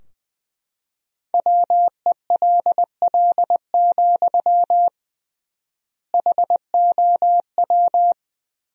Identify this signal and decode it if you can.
Morse code, heard as a keyed tone, about 20 wpm, 700 Hz. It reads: WELL, HOW
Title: 3x Word-Spacing - Sentences from Top 200 Words - 20wpm